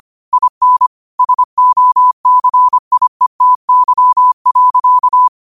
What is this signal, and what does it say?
Morse code, heard as a keyed tone, about 25 wpm, 1000 Hz. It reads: IN SOCIETY.
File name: 25wpm